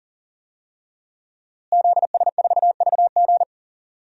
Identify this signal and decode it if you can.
Morse code, heard as a keyed tone, about 40 wpm, 700 Hz. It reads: ZS4VG
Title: DXCC-100 Call Signs - 5 Characters - Repeated 3x - 40wpm